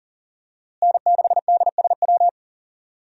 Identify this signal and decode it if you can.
Morse code, heard as a keyed tone, about 40 wpm, 700 Hz. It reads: N6DSW
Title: USA Call Signs - 5 Characters - Repeated 3x - 40wpm